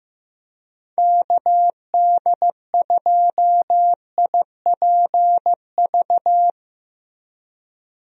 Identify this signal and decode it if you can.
Morse code, heard as a keyed tone, about 16 wpm, 700 Hz. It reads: KD2IPV
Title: USA Call Signs - 6 Characters - Repeated 3x - 15wpm